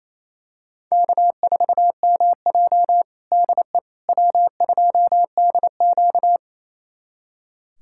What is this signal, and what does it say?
Morse code, heard as a keyed tone, about 28 wpm, 700 Hz. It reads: K4MJ DE W2DQ
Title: USA Callsign DE Callsign - 4 Characters - 28wpm